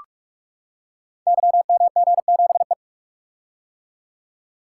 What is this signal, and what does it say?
Morse code, heard as a keyed tone, about 44 wpm, 700 Hz. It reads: YMG7E
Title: Code Groups - 5 Letter:Numbers - 45wpm